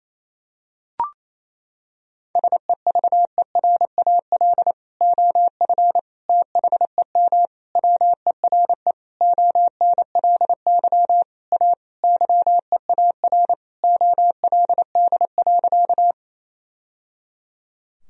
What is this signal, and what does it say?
Morse code, heard as a keyed tone, about 28 wpm, 700 Hz. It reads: SEVERAL OF THEM WERE ONLY A YEAR OLD.
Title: Sentences from Top 1000 Words - Review Part 2 - 28wpm